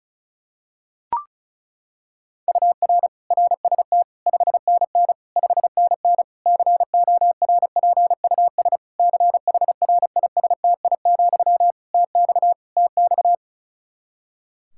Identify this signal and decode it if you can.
Morse code, heard as a keyed tone, about 36 wpm, 700 Hz. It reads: UR RST 5NN 5NN CORPUS CHRISTI, TX TX
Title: K1USN SST QSO Practice 35wpm